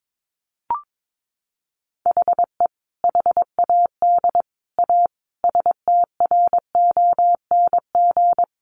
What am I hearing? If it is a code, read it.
Morse code, heard as a keyed tone, about 22 wpm, 700 Hz. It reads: HE HAD A STRONG
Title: Sentences from Top 1500 Words 22wpm